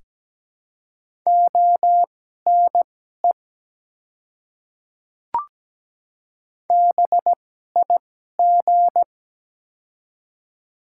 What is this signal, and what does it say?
Morse code, heard as a keyed tone, about 17 wpm, spaced as for 12 wpm, 700 Hz. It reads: ONE BIG
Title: 2x Character Spacing - Top 200 Words - 17wpm